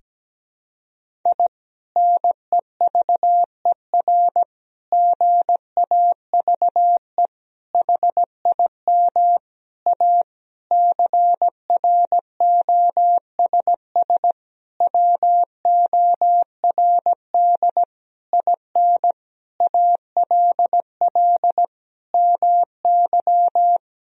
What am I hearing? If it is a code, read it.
Morse code, heard as a keyed tone, about 17 wpm, 700 Hz. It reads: I NEVER GAVE HIM A CROSS WORD IN ALL MY
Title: Sentences from Top 1000 Words - Review Part 1 - 17wpm